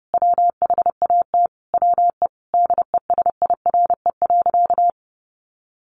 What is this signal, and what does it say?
Morse code, heard as a keyed tone, about 28 wpm, 700 Hz. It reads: WHAT WE DESIRE.
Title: Sentences from Top 1300 Words 30wpm